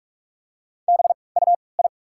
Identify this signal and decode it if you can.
Morse code, heard as a keyed tone, about 48 wpm, 690 Hz. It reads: B U I
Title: 3x Character Spacing - Top 200 Words - 45wpm